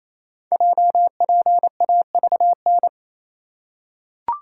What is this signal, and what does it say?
Morse code, heard as a keyed tone, about 27 wpm, 700 Hz. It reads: JPAVN
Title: Code Groups - 5 Letters - 28wpm